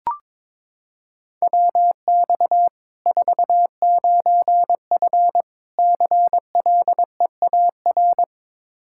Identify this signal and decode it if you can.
Morse code, heard as a keyed tone, about 22 wpm, 700 Hz. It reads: WX 49F CLEAR
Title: Practice QSO - Exchange 2 - Info - 22wpm